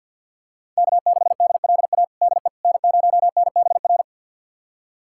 Morse code, heard as a keyed tone, about 50 wpm, 690 Hz. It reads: K6DLA DE N0NBR
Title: USA Callsign DE Callsign - 5 Characters - 50wpm